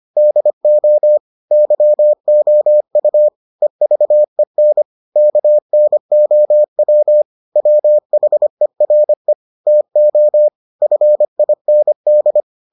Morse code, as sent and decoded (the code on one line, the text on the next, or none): -.. --- / -.-- --- ..- / . ...- . -. / -.- -. --- .-- / .-- .... .-. / - --- / ..-. .. -. -..
DO YOU EVEN KNOW WHERE TO FIND